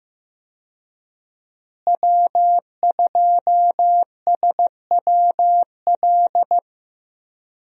Morse code, heard as a keyed tone, about 16 wpm, 700 Hz. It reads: W2SWL
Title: USA Call Signs - 5 Characters - Repeated 2x - 15wpm